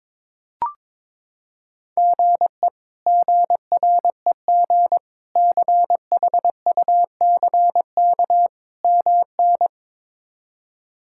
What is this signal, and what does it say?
Morse code, heard as a keyed tone, about 22 wpm, 700 Hz. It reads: GE GREG CHUCK MN
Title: K1USN SST QSO Practice - 22wpm